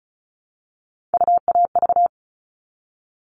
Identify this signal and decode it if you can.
Morse code, heard as a keyed tone, about 34 wpm, 700 Hz. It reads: UAV
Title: All Cut Numbers - 3-Digit 35wpm